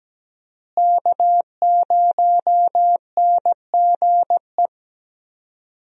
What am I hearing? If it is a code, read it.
Morse code, heard as a keyed tone, about 17 wpm, 700 Hz. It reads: K0NGE
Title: USA Call Signs - 5 Characters - Repeated 2x - 17wpm